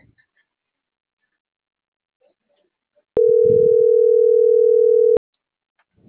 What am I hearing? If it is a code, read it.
Ringback tone (North America), 440 + 480 Hz.